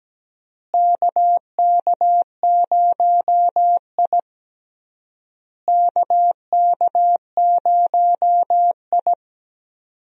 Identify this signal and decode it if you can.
Morse code, heard as a keyed tone, about 17 wpm, 700 Hz. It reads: KK0I KK0I
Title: USA Call Signs - 4 Characters - Repeated 3x - 17wpm